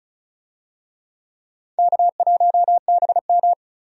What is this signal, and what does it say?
Morse code, heard as a keyed tone, about 36 wpm, 700 Hz. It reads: K1BM
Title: USA Call Signs - 4 Characters - 35wpm